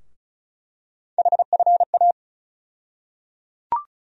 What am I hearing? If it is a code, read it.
Morse code, heard as a keyed tone, about 34 wpm, 700 Hz. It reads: HFA
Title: USA Call Sign Postfixes - 3 Letters 35wpm